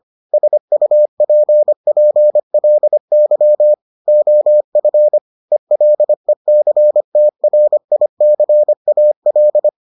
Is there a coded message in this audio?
Yes: Morse code from 0.3 to 9.7 s, heard as a keyed tone, about 25 wpm, 600 Hz. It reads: SUPPLY OF ELECTRICAL